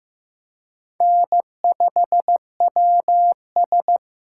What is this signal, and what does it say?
Morse code, heard as a keyed tone, about 15 wpm, 700 Hz. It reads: N5WS
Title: USA Call Signs - 4 Characters - 4x Character Spacing - 15wpm